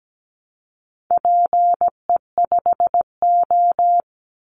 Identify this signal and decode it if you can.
Morse code, heard as a keyed tone, about 17 wpm, 700 Hz. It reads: PE5O
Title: Call Signs Repeated 3x - 17wpm